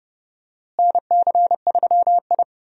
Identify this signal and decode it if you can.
Morse code, heard as a keyed tone, about 30 wpm, 700 Hz. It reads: NC3I
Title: USA Call Signs - 4 Characters - Repeated 2x - 30wpm